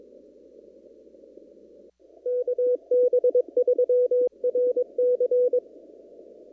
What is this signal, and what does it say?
Morse code, heard as a keyed tone, about 22 wpm, 500 Hz. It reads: KB3RC
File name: CW / kt5j wpx cw-op-k5pi